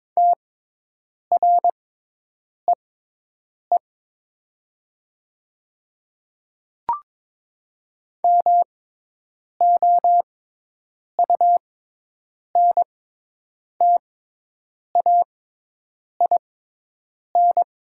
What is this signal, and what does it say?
Morse code, heard as a keyed tone, about 22 wpm, spaced as for 8 wpm, 700 Hz. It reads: TREE MOUNTAIN
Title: 6x Character Spacing - Top 300 Words - 22wpm